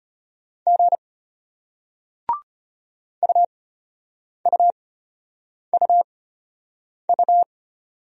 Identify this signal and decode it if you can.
Morse code, heard as a keyed tone, about 33 wpm, spaced as for 8 wpm, 700 Hz. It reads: G UUUU